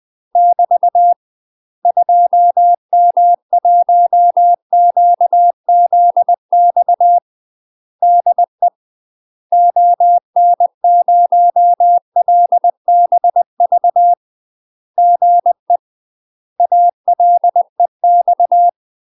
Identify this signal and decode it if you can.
Morse code, heard as a keyed tone, about 20 wpm, 700 Hz. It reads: <BT> 2M1QZX DE ON0LBV GE ALEX
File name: UK MASTER QSOs TRAINING/20WPM